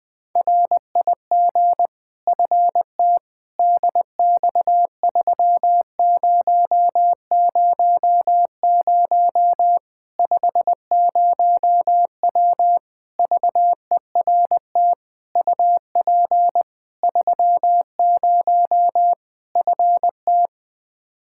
Morse code, heard as a keyed tone, about 20 wpm, 700 Hz. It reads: RIG FT DX3000 50W VERT UP 30 FT
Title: Practice QSO - Exchange 2 - Info - 20wpm